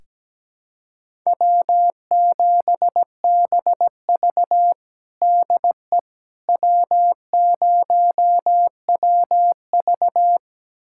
Morse code, heard as a keyed tone, about 17 wpm, 700 Hz. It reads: W7BV DE W0WV
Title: USA Callsign DE Callsign - 4 Characters - 17wpm